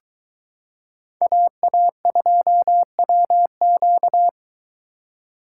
Morse code, heard as a keyed tone, about 23 wpm, 700 Hz. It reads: AA2WQ